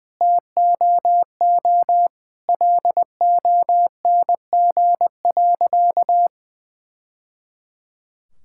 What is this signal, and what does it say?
Morse code, heard as a keyed tone, about 20 wpm, 700 Hz. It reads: TOO LONG.
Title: Sentences from Top 1000 Words - Review Part 5 - 20wpm